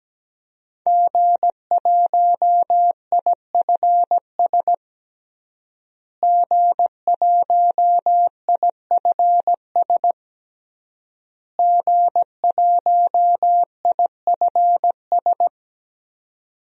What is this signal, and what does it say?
Morse code, heard as a keyed tone, about 17 wpm, 700 Hz. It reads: G1IFS G1IFS G1IFS
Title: DXCC-100 Call Signs - 5 Characters - Repeated 3x - 17wpm